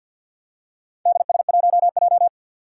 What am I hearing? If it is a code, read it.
Morse code, heard as a keyed tone, about 52 wpm, 700 Hz. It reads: DS1J
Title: Four Challenging Letter-Number Combinations 50wpm